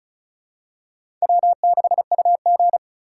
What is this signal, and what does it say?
Morse code, heard as a keyed tone, about 36 wpm, 700 Hz. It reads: W6UG
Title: USA Call Signs - 4 Characters - 4x Character Spacing - 35wpm